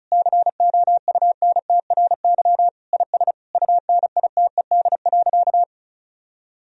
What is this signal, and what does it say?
Morse code, heard as a keyed tone, about 36 wpm, 700 Hz. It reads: COUNTRY IS UNITED.
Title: Sentences from Top 1000 Words - Review Part 4 - 35wpm